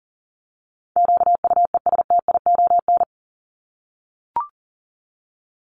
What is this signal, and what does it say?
Morse code, heard as a keyed tone, about 40 wpm, 700 Hz. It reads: QUESTION
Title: Top 500 Words Repeated 40wpm